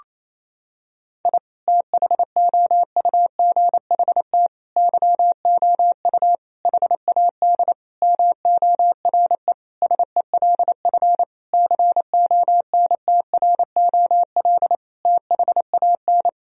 Morse code, heard as a keyed tone, about 28 wpm, 700 Hz. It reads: I THOUGHT YOU HAD MORE SELF CONTROL THAN
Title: Sentences from Top 1000 Words - Review Part 3 - 28wpm